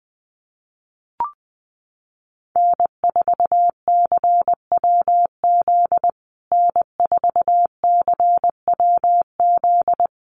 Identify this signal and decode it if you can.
Morse code, heard as a keyed tone, about 20 wpm, 700 Hz. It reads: N4CWZ N4CWZ
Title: USA Call Signs Repeated 3x 20wpm